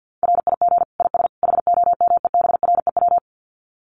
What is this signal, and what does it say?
Morse code, heard as a keyed tone, about 50 wpm, 700 Hz. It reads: RIG IS HOMEBREW
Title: Common QSO Phrases 50wpm